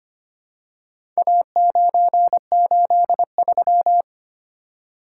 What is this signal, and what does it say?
Morse code, heard as a keyed tone, about 25 wpm, 700 Hz. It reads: A983
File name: practice / 25wpm / Four Letter-Number Combinations - 25wpm